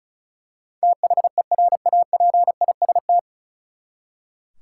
Morse code, heard as a keyed tone, about 34 wpm, 700 Hz. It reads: THERAPIST
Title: Practice QSO - Exchange 3 - More Info - 35wpm